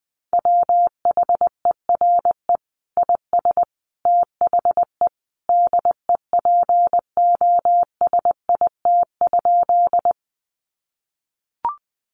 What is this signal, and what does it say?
Morse code, heard as a keyed tone, about 20 wpm, 700 Hz. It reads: WHERE IS THE DEPOSIT?
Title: Sentences from Top 1300 Words 20wpm